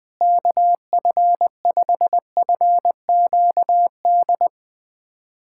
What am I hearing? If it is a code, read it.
Morse code, heard as a keyed tone, about 20 wpm, 700 Hz. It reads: KF5FQD
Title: USA Callsign DE Callsign - 6 Characters - 20wpm